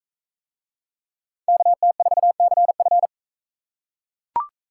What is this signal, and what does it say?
Morse code, heard as a keyed tone, about 43 wpm, 700 Hz. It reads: KT4CF